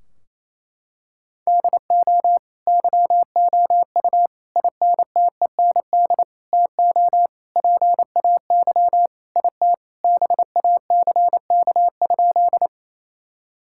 Morse code, heard as a keyed tone, about 28 wpm, 700 Hz. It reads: DO YOU INTEND TO PAY IT BACK?
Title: Sentences from Top 1600 Words 28wpm